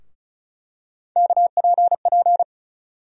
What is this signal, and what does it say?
Morse code, heard as a keyed tone, about 34 wpm, 700 Hz. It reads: KPP
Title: USA Call Sign Postfixes - 3 Letters 35wpm